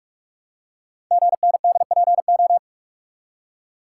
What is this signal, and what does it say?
Morse code, heard as a keyed tone, about 44 wpm, 690 Hz. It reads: GNDPO
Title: Code Groups - 5 Letters - 45wpm